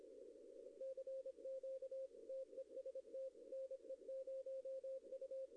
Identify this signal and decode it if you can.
Morse code, heard as a keyed tone, about 26 wpm, 530 Hz. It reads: CQ TEST NE0U
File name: contests / arrl-160-k5tr-op-w5tn -0100